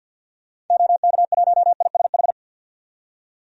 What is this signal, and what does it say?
Morse code, heard as a keyed tone, about 52 wpm, 700 Hz. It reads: OK1ISH
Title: DXCC-100 Call Signs - 6 Characters - 50wpm